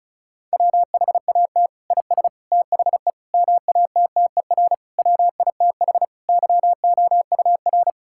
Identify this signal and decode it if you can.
Morse code, heard as a keyed tone, about 36 wpm, 700 Hz. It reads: WHAT IS THE MATTER WITH YOUR
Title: Sentences from Top 1000 Words - Review Part 5 - 35wpm